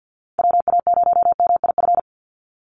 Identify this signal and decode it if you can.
Morse code, heard as a keyed tone, about 50 wpm, 700 Hz. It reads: WA0MIF
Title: USA Call Signs 50wpm